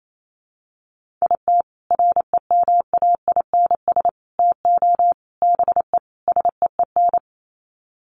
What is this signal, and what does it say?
Morse code, heard as a keyed tone, about 28 wpm, 700 Hz. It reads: IT REMAINS TO BE SEEN
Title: Sets of 5 Words Encore 28wpm